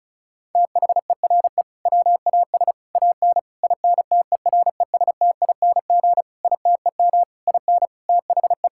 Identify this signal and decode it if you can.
Morse code, heard as a keyed tone, about 34 wpm, 700 Hz. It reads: THERE WAS AN INTERESTING ITEM IN THE
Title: Sentences from Top 1000 Words - Review Part 6 - 35wpm